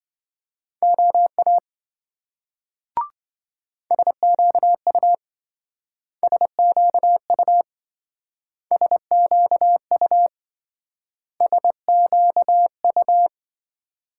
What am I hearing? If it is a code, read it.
Morse code, heard as a keyed tone, about 26 wpm, 700 Hz. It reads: OA SQU SQU SQU SQU